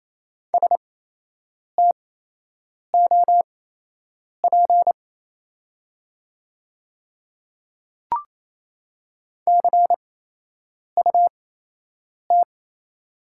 Morse code, heard as a keyed tone, about 28 wpm, spaced as for 8 wpm, 700 Hz. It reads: STOP CUT
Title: 8x Character Spacing - Top 500 Words - 28wpm